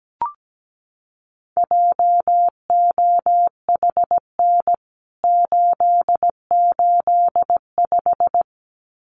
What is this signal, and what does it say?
Morse code, heard as a keyed tone, about 17 wpm, 700 Hz. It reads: JOHN 885
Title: CWT Exchange 17wpm